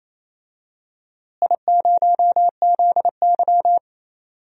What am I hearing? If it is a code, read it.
Morse code, heard as a keyed tone, about 28 wpm, 700 Hz. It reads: I0ZY
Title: DXCC-100 Call Signs - 4 Characters - Repeated 2x - 28wpm